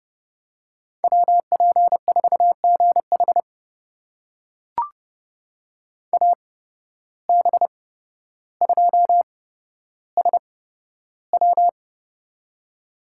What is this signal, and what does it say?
Morse code, heard as a keyed tone, about 30 wpm, spaced as for 13 wpm, 700 Hz. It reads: WP4GH AB2SW
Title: USA Call Signs - 5 Characters - 8x Character Spacing - 30wpm